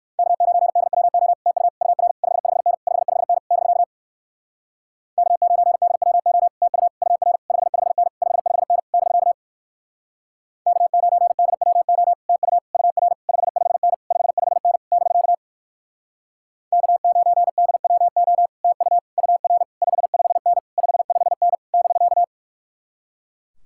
Morse code, heard as a keyed tone, about 58 wpm, 700 Hz. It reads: K9DWO TU UR 55N 55N <BK> K9DWO TU UR 55N 55N <BK> K9DWO TU UR 55N 55N <BK> K9DWO TU UR 55N 55N <BK>